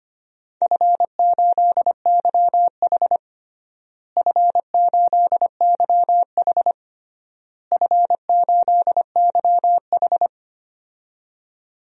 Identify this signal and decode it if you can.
Morse code, heard as a keyed tone, about 25 wpm, 700 Hz. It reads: F8YH F8YH F8YH
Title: DXCC-100 Call Signs - 4 Characters - Repeated 3x - 25wpm